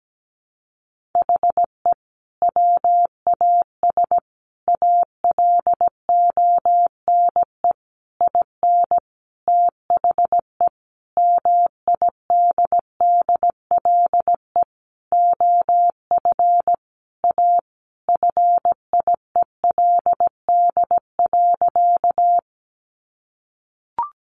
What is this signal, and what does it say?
Morse code, heard as a keyed tone, about 17 wpm, 700 Hz. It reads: HE WAS ALONE IN THE MIDDLE OF A FIELD.
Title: Sentences from Top 1000 Words 17wpm